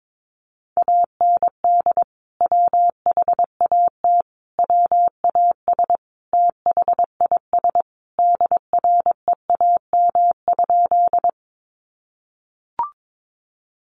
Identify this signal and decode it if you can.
Morse code, heard as a keyed tone, about 22 wpm, 700 Hz. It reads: AND WHAT WAS THIS DREAM?
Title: Sentences from Top 1500 Words 22wpm